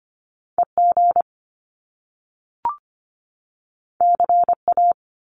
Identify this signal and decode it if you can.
Morse code, heard as a keyed tone, about 25 wpm, 700 Hz. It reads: EG CA
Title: Top Two Letter Combinations in Words 25wpm